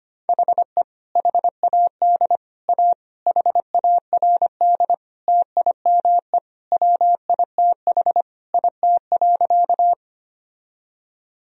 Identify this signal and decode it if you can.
Morse code, heard as a keyed tone, about 25 wpm, 700 Hz. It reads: HE HAD A HARD TIME WITH IT.
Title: Sentences from Top 1000 Words - Review Part 1 - 25wpm